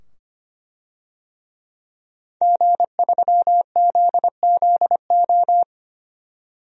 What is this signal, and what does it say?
Morse code, heard as a keyed tone, about 25 wpm, 700 Hz. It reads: G3ZZO
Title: DXCC-100 Call Signs - 5 Characters - 25wpm